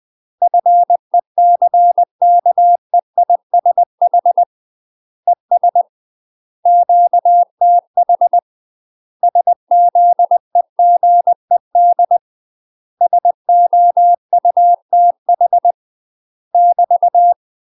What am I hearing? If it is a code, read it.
Morse code, heard as a keyed tone, about 20 wpm, 700 Hz. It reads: FECKEISH ES QTH SZEGED SOUTH <BT>
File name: UK MASTER QSOs TRAINING / 20WPM / UK MASTER QSOs 0055